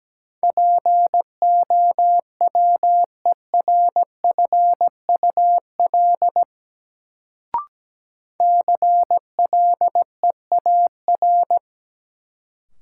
Morse code, heard as a keyed tone, about 17 wpm, 700 Hz. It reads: POWERFUL CLEAR
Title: Sets of 5 Words Encore - New Word Drill - 17wpm